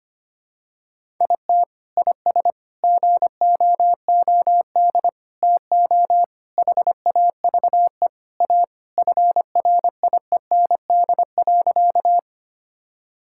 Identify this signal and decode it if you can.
Morse code, heard as a keyed tone, about 25 wpm, 700 Hz. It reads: IT IS GOOD TO HAVE A FRIEND.
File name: practice / 25wpm / Sentences with 3-Word Phrases - 25wpm